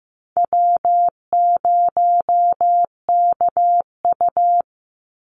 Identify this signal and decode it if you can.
Morse code, heard as a keyed tone, about 15 wpm, 700 Hz. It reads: W0KU
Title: Easy Call Signs Repeated 2x - 15wpm